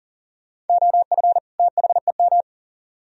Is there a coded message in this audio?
Yes: Morse code from 0.7 to 2.4 s, heard as a keyed tone, about 53 wpm, 700 Hz. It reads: OF THEM